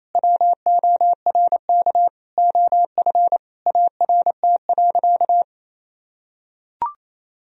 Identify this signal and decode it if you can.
Morse code, heard as a keyed tone, about 27 wpm, 700 Hz. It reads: WORK OF ART.
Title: Sentences from Top 1000 Words - Review Part 3 - 28wpm